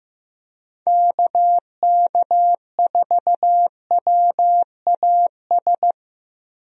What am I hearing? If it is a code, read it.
Morse code, heard as a keyed tone, about 15 wpm, 700 Hz. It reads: KK4WAS
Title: USA Call Signs - 6 Characters - Repeated 3x - 15wpm